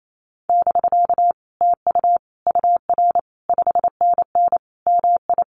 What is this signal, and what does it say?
Morse code, heard as a keyed tone, about 28 wpm, 700 Hz. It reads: <BK> TU UR 5NN MI
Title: Common QSO Phrases Encore 28wpm